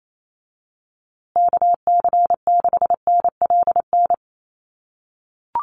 Morse code, heard as a keyed tone, about 28 wpm, 700 Hz. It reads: KC6NLN
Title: USA Call Signs Repeated 3x 28wpm